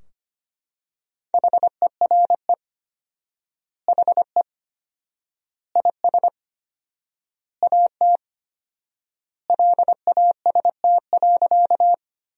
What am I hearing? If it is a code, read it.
Morse code, heard as a keyed tone, about 25 wpm, 700 Hz. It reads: HERE HE IS AT LAST.
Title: 4x Word-Spacing - Sentences from Top 300 Words - 25wpm